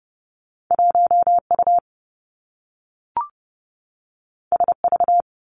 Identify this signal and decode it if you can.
Morse code, heard as a keyed tone, about 30 wpm, 700 Hz. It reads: 1 U S V
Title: Two Challenging Letter-Number Combinations 30wpm